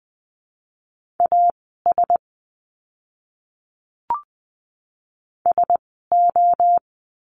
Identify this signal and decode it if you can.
Morse code, heard as a keyed tone, about 20 wpm, spaced as for 15 wpm, 700 Hz. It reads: AS SO
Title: 115 Lesson 19f - Letter w - Words 20wpm